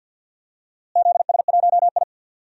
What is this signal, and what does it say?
Morse code, heard as a keyed tone, about 52 wpm, 700 Hz. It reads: ZS1I
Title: DXCC-100 Call Signs - 4 Characters - Repeated 3x - 50wpm